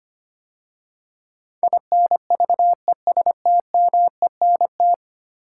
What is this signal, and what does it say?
Morse code, heard as a keyed tone, about 25 wpm, 700 Hz. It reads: INVESTMENT
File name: 10 Letter Words - First 3 Letters Given - 25wpm